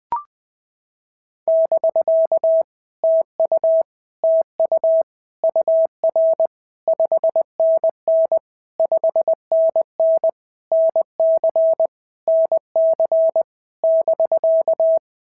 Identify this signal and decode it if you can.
Morse code, heard as a keyed tone, about 19 wpm, 650 Hz. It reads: <BK> TU TU UR 5NN 5NN NC NC <BK>
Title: POTA QSO Practice - 20wpm